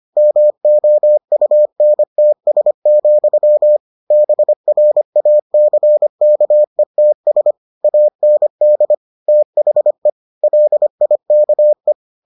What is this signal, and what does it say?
Morse code, heard as a keyed tone, about 25 wpm, 600 Hz. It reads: MOUNTS, BRACKETS AND THE LIKE